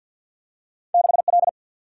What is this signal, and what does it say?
Morse code, heard as a keyed tone, about 52 wpm, 700 Hz. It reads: 6 L
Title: Single-Letter Single-Number 50wpm